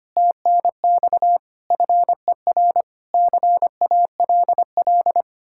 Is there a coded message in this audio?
Yes: Morse code from 0.2 to 5.2 s, heard as a keyed tone, about 33 wpm, 700 Hz. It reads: TNX FER CALL